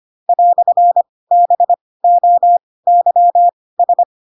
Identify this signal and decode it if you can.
Morse code, heard as a keyed tone, about 25 wpm, spaced as for 18 wpm, 700 Hz. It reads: "BOYS